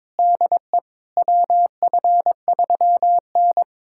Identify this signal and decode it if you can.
Morse code, heard as a keyed tone, about 22 wpm, 700 Hz. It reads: DE WF3N
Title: USA Callsign DE Callsign - 4 Characters - 22wpm